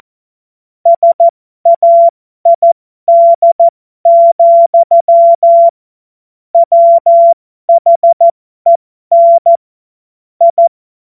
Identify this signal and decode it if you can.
Morse code, heard as a keyed tone, about 14 wpm, spaced as for 12 wpm, 680 Hz. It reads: SAID, WHEN I
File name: Morsekurs / Witze